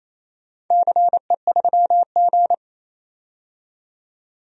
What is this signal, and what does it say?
Morse code, heard as a keyed tone, about 28 wpm, 700 Hz. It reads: CE3G
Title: DXCC-100 Call Signs - 4 Characters - 28wpm